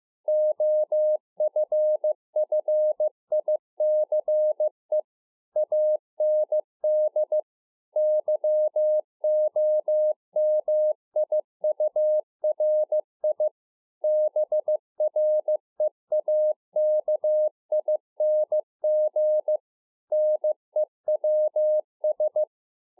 Morse code, heard as a keyed tone, about 15 wpm, 610 Hz. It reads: OFFICE AND YOMIURI BREAKING NEWS
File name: 2011-04-mp3 / THE 3.11 6 15WPM